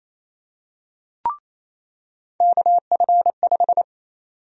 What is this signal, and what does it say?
Morse code, heard as a keyed tone, about 28 wpm, 700 Hz. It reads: KF5
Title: USA Call Sign Prefixes - 28wpm